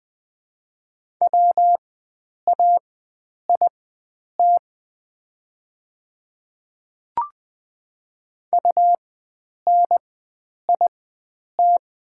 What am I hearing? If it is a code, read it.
Morse code, heard as a keyed tone, about 20 wpm, spaced as for 9 wpm, 700 Hz. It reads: WAIT UNIT
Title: 4x Character Spacing - Top 500 Words - 20wpm